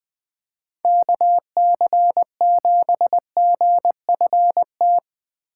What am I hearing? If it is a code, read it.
Morse code, heard as a keyed tone, about 24 wpm, 700 Hz. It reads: KC7GFT
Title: Speed Racing - USA 2x3 Call Signs 20wpm